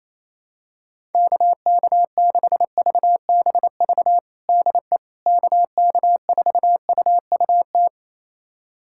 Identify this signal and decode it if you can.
Morse code, heard as a keyed tone, about 27 wpm, 700 Hz. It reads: KK6VBV DE KK4UUT
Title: USA Callsign DE Callsign - 6 Characters - 28wpm